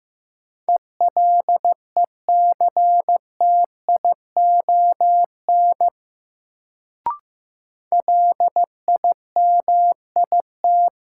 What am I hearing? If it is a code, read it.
Morse code, heard as a keyed tone, about 15 wpm, 700 Hz. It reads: ELECTION LIMIT
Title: Sentences from Top 1500 Words - New Word Drill Alt - 15wpm